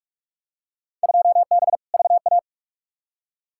Morse code, heard as a keyed tone, about 44 wpm, 700 Hz. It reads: 2B VA
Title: Field Day Exchanges Encore - 45wpm